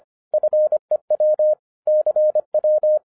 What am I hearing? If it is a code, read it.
Morse code, heard as a keyed tone, about 25 wpm, 610 Hz. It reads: FEW CW
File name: 2012-01-mp3 / n06-FIRST FISTS CONTACT-25wpm